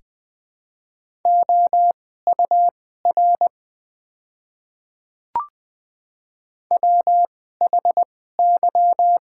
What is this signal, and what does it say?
Morse code, heard as a keyed tone, about 20 wpm, spaced as for 14 wpm, 700 Hz. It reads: OUR WHY
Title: 2x Character Spacing - Top 200 Words - 20wpm